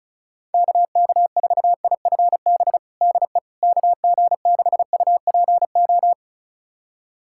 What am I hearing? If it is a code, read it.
Morse code, heard as a keyed tone, about 34 wpm, 700 Hz. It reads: KK4IFB DE KG6UPO
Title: USA Callsign DE Callsign - 6 Characters - 35wpm